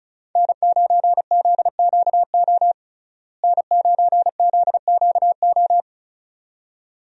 Morse code, heard as a keyed tone, about 34 wpm, 700 Hz. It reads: N9ZQO N9ZQO
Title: USA Call Signs - 5 Characters - Repeated 2x - 35wpm